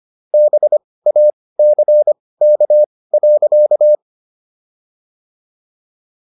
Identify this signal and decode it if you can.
Morse code, heard as a keyed tone, about 25 wpm, spaced as for 18 wpm, 600 Hz. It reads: BACK.
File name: SS123 18